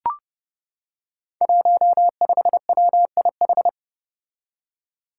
Morse code, heard as a keyed tone, about 30 wpm, 700 Hz. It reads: 15WIH
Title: Five Challenging Letter-Number Combinations 30wpm